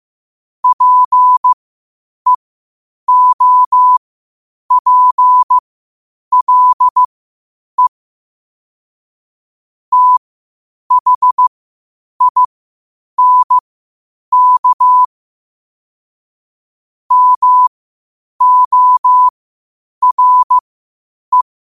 Morse code, heard as a keyed tone, about 15 wpm, spaced as for 8 wpm, 1000 Hz. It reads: PEOPLE THINK MORE
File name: 8wpm